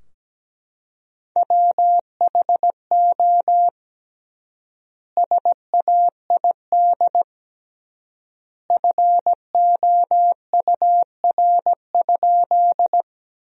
Morse code, heard as a keyed tone, about 17 wpm, 700 Hz. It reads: WHO SAID FOUR?
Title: 3x Word-Spacing - Sentences from Top 300 Words - 17wpm